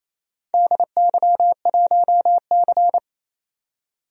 Morse code, heard as a keyed tone, about 28 wpm, 700 Hz. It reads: DY1C
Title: DXCC-100 Call Signs - 4 Characters - 28wpm